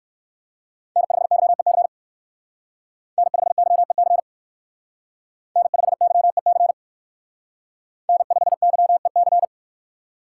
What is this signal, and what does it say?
Morse code, heard as a keyed tone, about 60 wpm, 700 Hz. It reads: N5YEC N5YEC N5YEC N5YEC